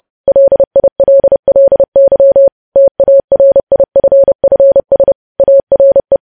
Morse code, heard as a keyed tone, about 30 wpm, 550 Hz. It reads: LILLY TARIFFS ARE
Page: News Headlines in Morse Code at 30 WPM